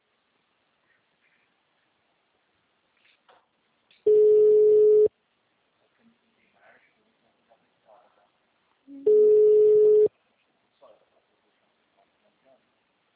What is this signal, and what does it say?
Ringback tone (Europe, 425 Hz), with speech.
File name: out- -2013- - - .43